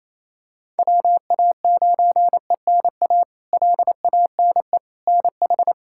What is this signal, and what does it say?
Morse code, heard as a keyed tone, about 28 wpm, 700 Hz. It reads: WA9ENA LANE NH
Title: K1USN SST QSO Practice - 28wpm